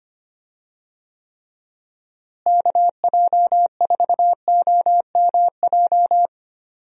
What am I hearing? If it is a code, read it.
Morse code, heard as a keyed tone, about 25 wpm, 700 Hz. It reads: KJ4OMJ
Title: USA Call Signs - 6 Characters - Repeated 2x - 25wpm